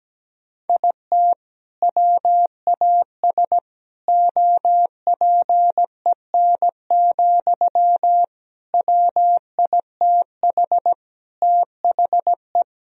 Morse code, heard as a keyed tone, about 17 wpm, 700 Hz. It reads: IT WAS OPEN, WITH THE
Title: Sentences from Top 1000 Words - Review Part 4 - 17wpm